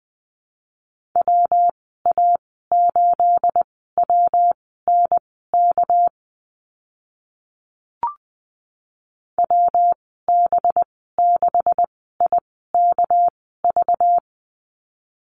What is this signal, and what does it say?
Morse code, heard as a keyed tone, about 20 wpm, spaced as for 15 wpm, 700 Hz. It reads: WA8WNK WB6IKV
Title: 213 Lesson 34g - Number 8 - Call Signs 20wpm